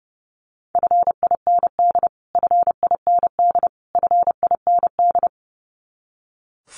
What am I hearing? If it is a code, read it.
Morse code, heard as a keyed tone, about 30 wpm, 700 Hz. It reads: FIND FIND FIND
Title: Top 200 Words Repeated 30wpm